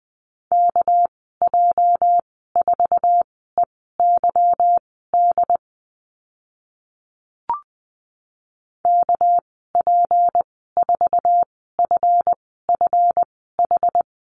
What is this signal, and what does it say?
Morse code, heard as a keyed tone, about 20 wpm, spaced as for 15 wpm, 700 Hz. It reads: KJ4EYD KP4FFH
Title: 219 Lesson 35g - Number 0 - Call Signs 20wpm